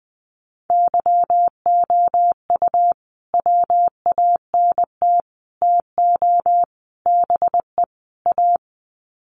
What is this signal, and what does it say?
Morse code, heard as a keyed tone, about 20 wpm, 700 Hz. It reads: YOU WANT TO BE A
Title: Sets of 5 Words Encore2 20wpm